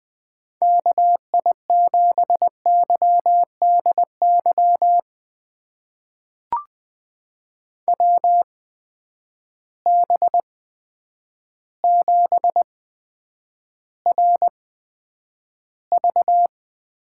Morse code, heard as a keyed tone, about 20 wpm, spaced as for 9 wpm, 700 Hz. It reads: KI7YDY WB7RV
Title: USA Call Signs - 6 Characters - 8x Character Spacing - 20wpm